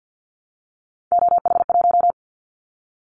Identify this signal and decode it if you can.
Morse code, heard as a keyed tone, about 50 wpm, 700 Hz. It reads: OH1
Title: Callsign Prefixes for Top 25 DXCC - 50wpm